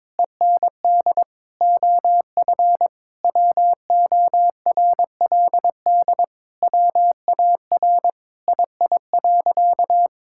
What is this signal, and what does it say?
Morse code, heard as a keyed tone, about 22 wpm, 700 Hz. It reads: END OF WORLD WAR II.
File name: Sentences with 7-Word Phrases - 22wpm